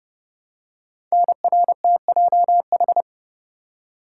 Morse code, heard as a keyed tone, about 30 wpm, 700 Hz. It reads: NRTJH
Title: Code Groups - 5 Letters - 30wpm